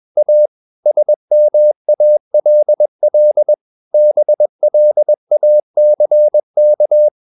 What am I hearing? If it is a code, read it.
Morse code, heard as a keyed tone, about 21 wpm, 600 Hz. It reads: A SMALL BLACK
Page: CW med Gnister